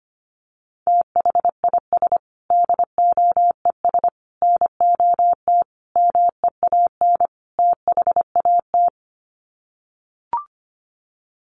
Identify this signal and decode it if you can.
Morse code, heard as a keyed tone, about 25 wpm, 700 Hz. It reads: THIS DOES NOT MEAN THAT
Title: Sets of 5 Words Encore2 25wpm